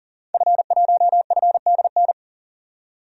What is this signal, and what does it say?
Morse code, heard as a keyed tone, about 40 wpm, 700 Hz. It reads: F1FDN
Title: DXCC-100 Call Signs - 5 Characters - 40wpm